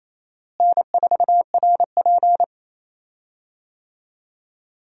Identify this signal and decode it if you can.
Morse code, heard as a keyed tone, about 28 wpm, 700 Hz. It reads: N4RP
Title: USA Call Signs - 4 Characters - Repeated 3x - 28wpm